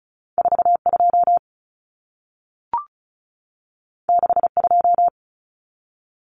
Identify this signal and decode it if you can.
Morse code, heard as a keyed tone, about 36 wpm, 700 Hz. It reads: 42 62
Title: Common Cut Numbers - 2-Digit 35wpm